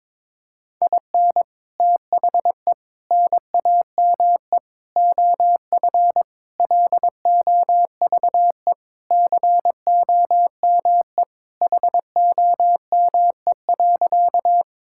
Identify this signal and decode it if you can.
Morse code, heard as a keyed tone, about 22 wpm, 700 Hz. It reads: IN THE NAME OF LOVE COME HOME.
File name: practice / 22wpm / Sentences with 3-Word Phrases - 22wpm